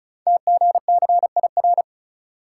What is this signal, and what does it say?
Morse code, heard as a keyed tone, about 36 wpm, 690 Hz. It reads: TGCIR
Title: Code Groups - 5 Letters - 35wpm